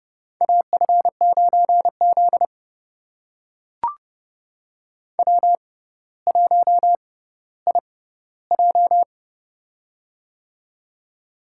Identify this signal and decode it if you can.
Morse code, heard as a keyed tone, about 32 wpm, spaced as for 16 wpm, 700 Hz. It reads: AF9Z W1IJ
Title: USA Call Signs - 4 Characters - 6x Character Spacing - 30wpm